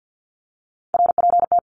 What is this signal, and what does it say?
Morse code, heard as a keyed tone, about 50 wpm, 700 Hz. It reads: RPT
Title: Abbreviations 50wpm